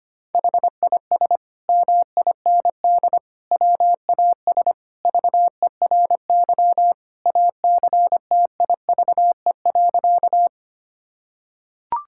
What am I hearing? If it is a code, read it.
Morse code, heard as a keyed tone, about 25 wpm, 700 Hz. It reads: HIS MIND WAS VERY ACTIVE.
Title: Sentences from Top 1000 Words - Review Part 6 - 25wpm